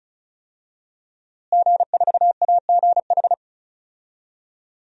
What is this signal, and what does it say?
Morse code, heard as a keyed tone, about 34 wpm, 700 Hz. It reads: G4AGH
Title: DXCC-100 Call Signs - 5 Characters - Repeated 3x - 35wpm